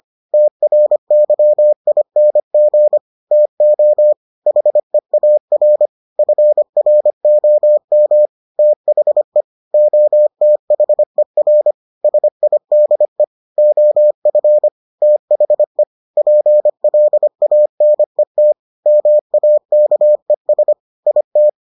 Morse code, heard as a keyed tone, about 25 wpm, 600 Hz. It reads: TRYING TO HEAR FROM THE OTHER SIDE OF THE PLANET MAKES IT